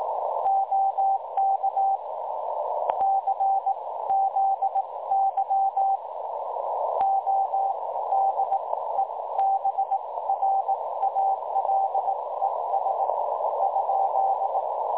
Morse code, heard as a keyed tone, about 20 wpm, spaced as for 16 wpm, 780 Hz, with noise. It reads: OK CZY ODEBRALES ?
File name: dlazaawansowanych / nagraniazpasma / websdr recording 3565.0kHz